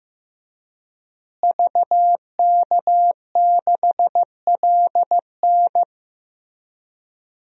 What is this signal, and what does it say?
Morse code, heard as a keyed tone, about 15 wpm, 700 Hz. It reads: VK6LN
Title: DXCC-100 Call Signs - 5 Characters - Repeated 3x - 15wpm